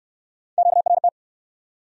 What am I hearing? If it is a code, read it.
Morse code, heard as a keyed tone, about 58 wpm, 700 Hz. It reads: QRT